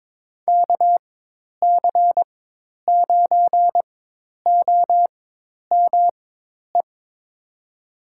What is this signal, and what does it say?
Morse code, heard as a keyed tone, about 22 wpm, spaced as for 10 wpm, 700 Hz. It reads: KC9OME
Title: USA Call Signs - 4x Character Spacing - 22wpm